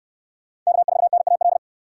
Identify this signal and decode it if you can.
Morse code, heard as a keyed tone, about 62 wpm, 700 Hz. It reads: K4NAL